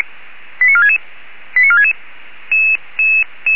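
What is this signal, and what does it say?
Five-tone selective call (ZVEI), heard as quick runs of five tones, tones of about 80 ms. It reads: 89466 89466
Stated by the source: ZVEI five-tone call 89332 received by SDR.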